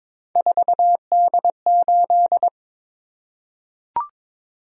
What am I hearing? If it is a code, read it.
Morse code, heard as a keyed tone, about 22 wpm, 700 Hz. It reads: 4D8
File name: Three Letter-Number Combinations - 22wpm